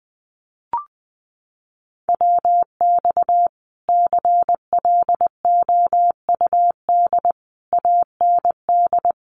Morse code, heard as a keyed tone, about 19 wpm, 700 Hz. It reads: WX CLOUD AND
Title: Common QSO Phrases Encore 20wpm